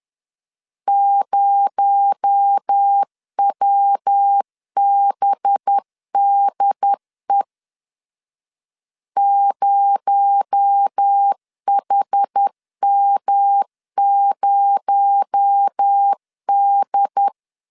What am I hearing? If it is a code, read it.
Morse code, heard as a keyed tone, about 10 wpm, 800 Hz. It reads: 0WBDE 0HM0D
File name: LEZIONI / 6 PKX0 / 6 PKX0 10wpm 2707